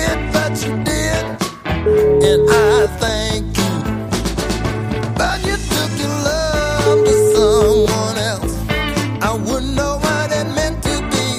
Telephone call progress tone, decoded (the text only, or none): ringback tone